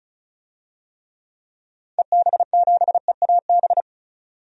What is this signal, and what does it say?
Morse code, heard as a keyed tone, about 34 wpm, 700 Hz. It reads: EB7EAB
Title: DXCC-100 Call Signs - 6 Characters - 35wpm